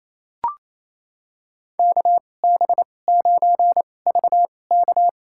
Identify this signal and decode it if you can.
Morse code, heard as a keyed tone, about 28 wpm, spaced as for 20 wpm, 700 Hz. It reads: KB9VK
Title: USA Call Signs - 5 Characters - 2x Character Spacing - 28wpm